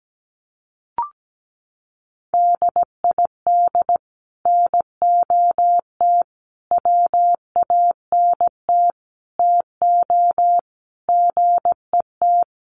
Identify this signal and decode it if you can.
Morse code, heard as a keyed tone, about 17 wpm, 700 Hz. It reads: DID NOT WANT TO GET
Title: Sets of 5 Words Encore2 17wpm